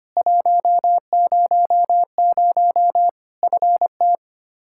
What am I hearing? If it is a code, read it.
Morse code, heard as a keyed tone, about 25 wpm, 700 Hz. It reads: 100 FT
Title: Practice QSO - Exchange 2 - Info - 25wpm